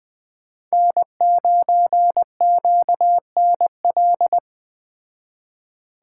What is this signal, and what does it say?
Morse code, heard as a keyed tone, about 21 wpm, 700 Hz. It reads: N9QNL